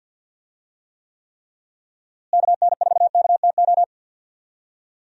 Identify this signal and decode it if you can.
Morse code, heard as a keyed tone, about 50 wpm, 700 Hz. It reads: KN4KTO
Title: USA Call Signs - 6 Characters - Repeated 3x - 50wpm